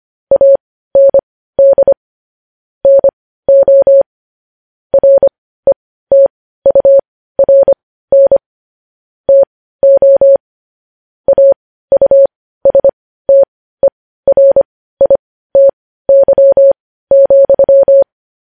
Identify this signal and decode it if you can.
Morse code, heard as a keyed tone, about 25 wpm, spaced as for 15 wpm, 550 Hz. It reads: AND NO RETURN TO AUSTERITY,
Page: News Headlines in Morse Code at 15 WPM